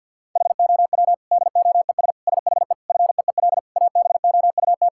Morse code, heard as a keyed tone, about 50 wpm, 700 Hz. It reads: HOW DOES SHE FEEL ABOUT
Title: Sentences from Top 1000 Words - Review Part 6 - 50wpm